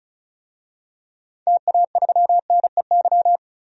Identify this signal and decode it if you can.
Morse code, heard as a keyed tone, about 36 wpm, 700 Hz. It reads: TA3NEY
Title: DXCC-100 Call Signs - 6 Characters - 35wpm